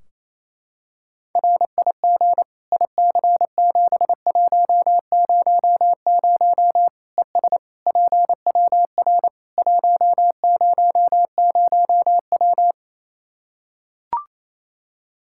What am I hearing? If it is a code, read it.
Morse code, heard as a keyed tone, about 28 wpm, 700 Hz. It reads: RIG IC7100 ES PWR 100W
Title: Practice QSO - Exchange 2 - Info - 28wpm